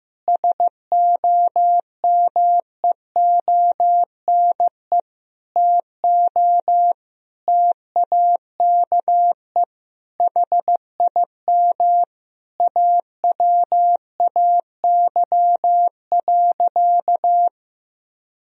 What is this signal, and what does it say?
Morse code, heard as a keyed tone, about 16 wpm, 700 Hz. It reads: SOMEONE TO TAKE HIM AWAY.
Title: Sentences from Top 1000 Words - Review Part 4 - 15wpm